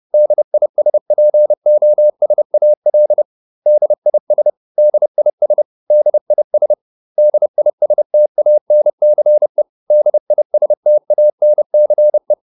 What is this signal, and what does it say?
Morse code, heard as a keyed tone, about 30 wpm, 600 Hz. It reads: DISPOSAL DIS DIS DIS DISTANCE DISTANCE